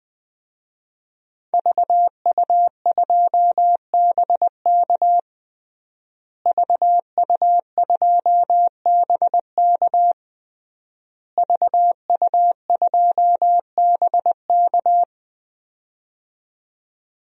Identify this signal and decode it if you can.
Morse code, heard as a keyed tone, about 21 wpm, 700 Hz. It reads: VU2BK VU2BK VU2BK
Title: DXCC-100 Call Signs - 5 Characters - Repeated 3x - 20wpm